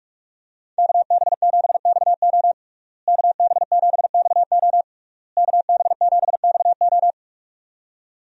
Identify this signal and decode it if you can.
Morse code, heard as a keyed tone, about 46 wpm, 700 Hz. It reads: KB7XO KB7XO KB7XO
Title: USA Call Signs - 5 Characters - Repeated 3x - 45wpm